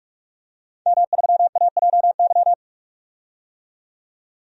Morse code, heard as a keyed tone, about 46 wpm, 700 Hz. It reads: M3AJY
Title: DXCC-100 Call Signs - 5 Characters - 45wpm